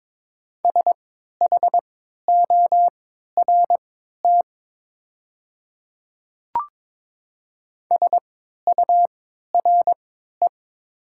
Morse code, heard as a keyed tone, about 22 wpm, spaced as for 13 wpm, 700 Hz. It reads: SHORT SURE
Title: 3x Character Spacing - Top 500 Words - 22wpm